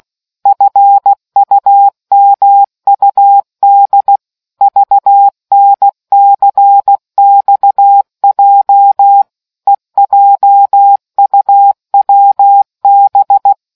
Morse code, heard as a keyed tone, about 16 wpm, 800 Hz. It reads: FUMUD VNCXJ EJUWB